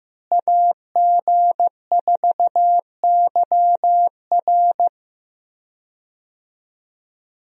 Morse code, heard as a keyed tone, about 15 wpm, 700 Hz. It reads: AG4YR
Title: USA Call Signs - 5 Characters - Repeated 2x - 15wpm